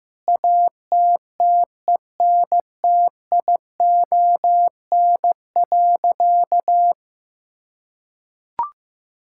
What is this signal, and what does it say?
Morse code, heard as a keyed tone, about 15 wpm, 700 Hz. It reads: ATTENTION.
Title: Sentences from Top 1000 Words - Review Part 6 - 15wpm